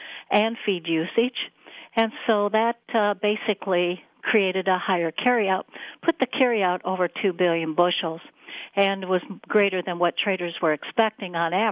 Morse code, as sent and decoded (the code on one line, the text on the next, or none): none